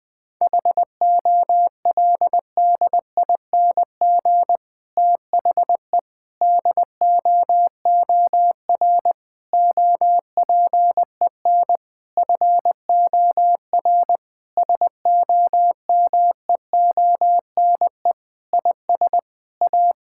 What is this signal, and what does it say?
Morse code, heard as a keyed tone, about 21 wpm, 700 Hz. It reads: HOLDING THE DOOR OPEN FOR SOMEONE IS A
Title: Difficult Sentences with Hints - 20wpm